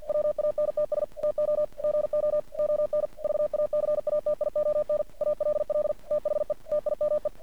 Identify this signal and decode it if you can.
Morse code, heard as a keyed tone, about 48 wpm, 600 Hz. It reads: WANTS TO GO ON VACATION ALL THE TIME
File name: archived / CRSnet